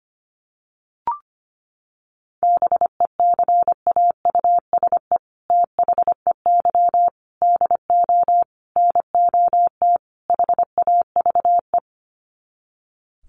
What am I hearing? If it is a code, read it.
Morse code, heard as a keyed tone, about 25 wpm, 700 Hz. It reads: BECAUSE THEY DO NOT HAVE
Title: Sets of 5 Words Encore2 25wpm